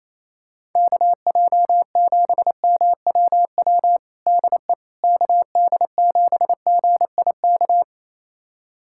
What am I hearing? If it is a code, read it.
Morse code, heard as a keyed tone, about 28 wpm, 700 Hz. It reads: KJ7MWW DE KD7GIK
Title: USA Callsign DE Callsign - 6 Characters - 28wpm